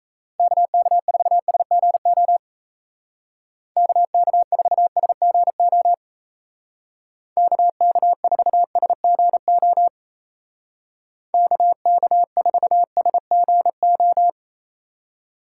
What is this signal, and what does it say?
Morse code, heard as a keyed tone, about 34 wpm, 700 Hz. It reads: KK4SGO KK4SGO KK4SGO KK4SGO
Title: Speed Racing - USA 2x3 Call Signs 28wpm